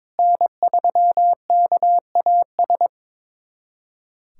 Morse code, heard as a keyed tone, about 22 wpm, 700 Hz. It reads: N3KAS
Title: USA Callsign DE Callsign 22wpm